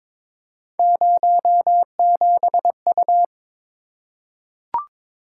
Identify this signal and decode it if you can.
Morse code, heard as a keyed tone, about 22 wpm, 700 Hz. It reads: 07U
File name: practice / 22wpm / Three Letter-Number Combinations - 22wpm